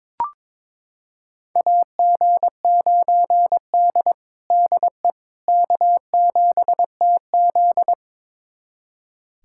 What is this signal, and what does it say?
Morse code, heard as a keyed tone, about 21 wpm, 700 Hz. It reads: AG9D DE K7TZ
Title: USA Callsign DE Callsign - 4 Characters - 22wpm